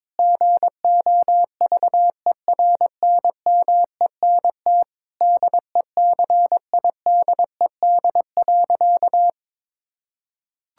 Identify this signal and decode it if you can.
Morse code, heard as a keyed tone, about 22 wpm, 700 Hz. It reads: GOVERNMENT DECIDED.
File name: practice / 22wpm / Sentences with 3-Word Phrases - 22wpm